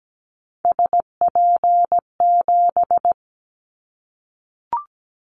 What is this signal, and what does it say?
Morse code, heard as a keyed tone, about 17 wpm, 700 Hz. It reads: SP7
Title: Callsign Prefixes for Top 25 DXCC - 17wpm